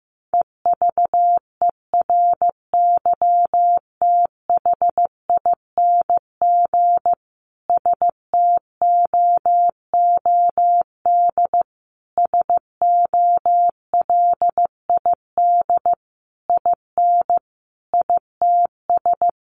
Morse code, heard as a keyed tone, about 15 wpm, 700 Hz. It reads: EVERYTHING STOOD SOLID IN ITS
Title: Sentences from Top 1500 Words 15wpm